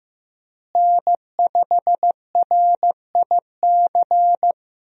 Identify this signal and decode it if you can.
Morse code, heard as a keyed tone, about 15 wpm, 700 Hz. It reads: N5RIC
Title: USA Call Signs - 5 Characters - Repeated 3x - 15wpm